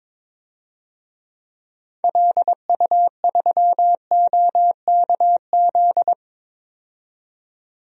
Morse code, heard as a keyed tone, about 22 wpm, 700 Hz. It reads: LU3OKZ
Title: DXCC-100 Call Signs - 6 Characters - 22wpm